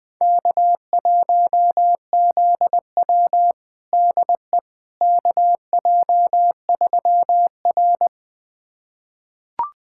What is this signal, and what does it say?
Morse code, heard as a keyed tone, about 21 wpm, 700 Hz. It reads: K1ZW DE KJ3R
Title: USA Callsign DE Callsign - 4 Characters - 20wpm